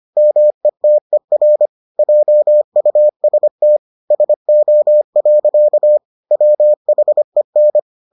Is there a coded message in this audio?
Yes: Morse code from 0.2 to 7.8 s, heard as a keyed tone, about 25 wpm, 600 Hz. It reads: METER JUST SO. WHEN